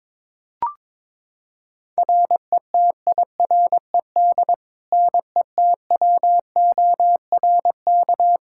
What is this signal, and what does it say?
Morse code, heard as a keyed tone, about 22 wpm, 700 Hz. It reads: RETIRED NETWORK
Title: Practice QSO - Exchange 3 - More Info - 22wpm